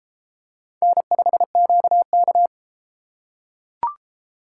Morse code, heard as a keyed tone, about 32 wpm, 700 Hz. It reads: N5QK